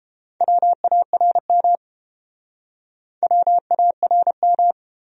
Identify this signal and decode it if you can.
Morse code, heard as a keyed tone, about 31 wpm, 700 Hz. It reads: WARM WARM